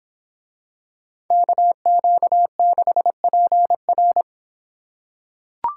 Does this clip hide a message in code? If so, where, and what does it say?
Yes: Morse code from 1.3 to 4.2 s, heard as a keyed tone, about 26 wpm, 700 Hz. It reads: KQ6PR